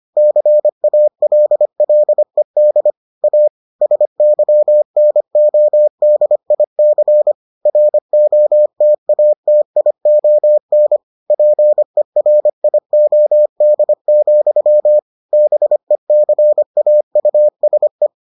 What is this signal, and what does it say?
Morse code, heard as a keyed tone, about 25 wpm, 600 Hz. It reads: CALLED A SYNODIC ROTATION PERIOD, BECAUSE